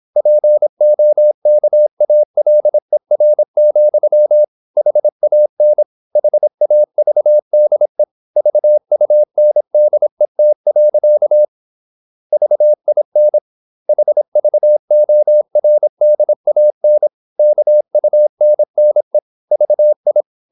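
Morse code, heard as a keyed tone, about 26 wpm, 600 Hz. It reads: POKALER, HAN HAVDE VUNDET. VIN HVORDAN KUNNE VI
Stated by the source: High-speed (QRQ) Morse at 26 wpm.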